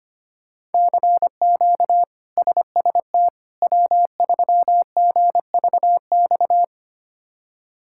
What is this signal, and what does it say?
Morse code, heard as a keyed tone, about 25 wpm, 700 Hz. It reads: CQ SST W3GVX
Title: K1USN SST QSO Practice - 25wpm